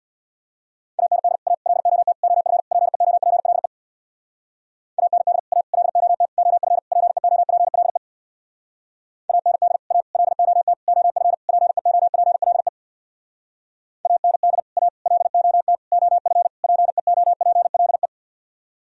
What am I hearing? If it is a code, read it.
Morse code, heard as a keyed tone, about 64 wpm, 700 Hz. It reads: AND A LOT OF PEOPLE AND A LOT OF PEOPLE AND A LOT OF PEOPLE AND A LOT OF PEOPLE